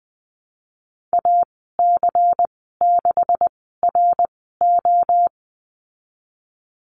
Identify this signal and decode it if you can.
Morse code, heard as a keyed tone, about 20 wpm, spaced as for 15 wpm, 700 Hz. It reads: AC6RO
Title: 140 Lesson 23g - Letter f - Call Signs 20wpm